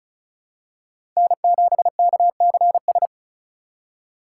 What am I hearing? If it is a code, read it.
Morse code, heard as a keyed tone, about 34 wpm, 700 Hz. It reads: N7KCS
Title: USA Call Signs - 5 Characters - 4x Character Spacing - 35wpm